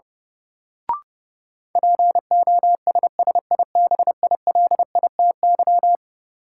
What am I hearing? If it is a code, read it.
Morse code, heard as a keyed tone, about 30 wpm, 700 Hz. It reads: POSSIBILITY
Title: Sets of 3 Words - New Word Drill - 30wpm